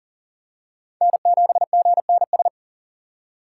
Morse code, heard as a keyed tone, about 40 wpm, 700 Hz. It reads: N7GNS
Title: USA Call Signs - 6x Character Spacing - 40wpm